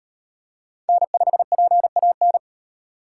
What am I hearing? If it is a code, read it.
Morse code, heard as a keyed tone, about 38 wpm, 700 Hz. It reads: N5PAN